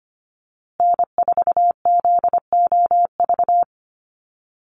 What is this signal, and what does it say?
Morse code, heard as a keyed tone, about 25 wpm, 700 Hz. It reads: N4ZOV
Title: USA Call Signs Repeated 3x 25wpm